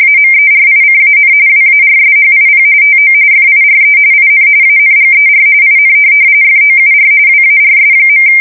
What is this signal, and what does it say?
RTTY, about 45.45 baud. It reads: AMERICAN WARSHIP TO FIRE IN ANGER IN FOUR WARS.
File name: fsk170 / USS-New-Jersey